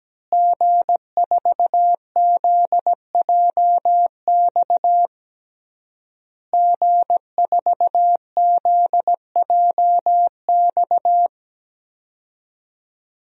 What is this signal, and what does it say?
Morse code, heard as a keyed tone, about 17 wpm, 700 Hz. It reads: G4ZJX G4ZJX
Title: DXCC-100 Call Signs - 5 Characters - Repeated 3x - 17wpm